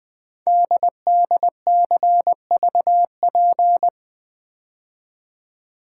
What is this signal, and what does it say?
Morse code, heard as a keyed tone, about 20 wpm, 700 Hz. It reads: DDCVP
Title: Code Groups - 5 Letters - 20wpm